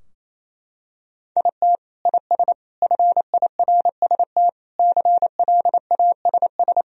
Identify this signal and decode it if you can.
Morse code, heard as a keyed tone, about 27 wpm, 700 Hz. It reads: IT IS FIRST CLASS
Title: Sentences from Top 1000 Words - Review Part 3 - 28wpm